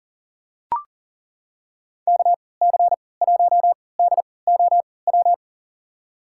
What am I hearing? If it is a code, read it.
Morse code, heard as a keyed tone, about 40 wpm, spaced as for 23 wpm, 700 Hz. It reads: KC1DOW
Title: USA Call Signs - 6 Characters - 3x Character Spacing - 40wpm